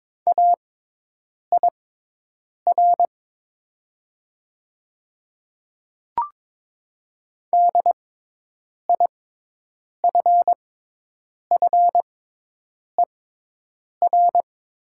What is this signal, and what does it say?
Morse code, heard as a keyed tone, about 22 wpm, spaced as for 8 wpm, 700 Hz. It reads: AIR DIFFER
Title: 6x Character Spacing - Top 200 Words - 22wpm